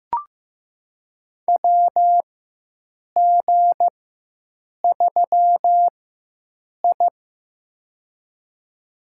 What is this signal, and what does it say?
Morse code, heard as a keyed tone, about 16 wpm, spaced as for 7 wpm, 700 Hz. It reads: WG3I
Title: USA Call Signs - 4x Character Spacing - 15wpm